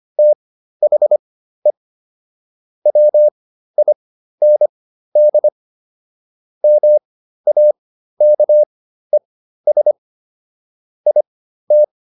Morse code, heard as a keyed tone, about 25 wpm, spaced as for 13 wpm, 600 Hz. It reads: THE WIND MAKES IT